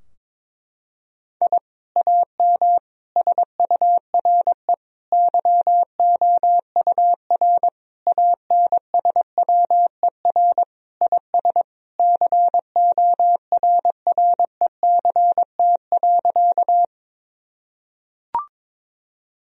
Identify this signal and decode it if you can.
Morse code, heard as a keyed tone, about 21 wpm, 700 Hz. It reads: I AM SURE YOUR ANSWER IS CORRECT.
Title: Sentences from Top 1000 Words - Review Part 2 - 22wpm